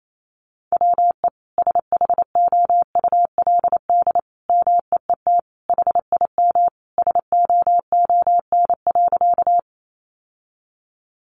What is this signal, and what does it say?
Morse code, heard as a keyed tone, about 28 wpm, 700 Hz. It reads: WE SHOULD MEET HIM SOON.
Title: Sentences from Top 1100 Words 28wpm